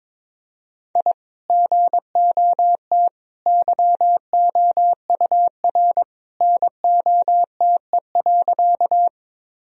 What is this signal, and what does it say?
Morse code, heard as a keyed tone, about 22 wpm, 700 Hz. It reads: I GOT YOUR NOTE.
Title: Sentences from Top 1000 Words - Review Part 2 - 22wpm